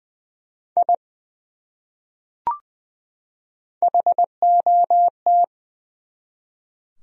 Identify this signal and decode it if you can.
Morse code, heard as a keyed tone, about 20 wpm, 700 Hz. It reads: I HOT
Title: Top 100 Words 20wpm